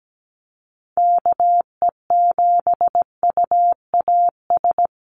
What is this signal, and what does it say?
Morse code, heard as a keyed tone, about 17 wpm, 700 Hz. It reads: KE7UAS
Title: USA Call Signs Repeated 3x 17wpm